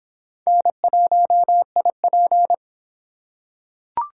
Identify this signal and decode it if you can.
Morse code, heard as a keyed tone, about 25 wpm, 700 Hz. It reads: N1IP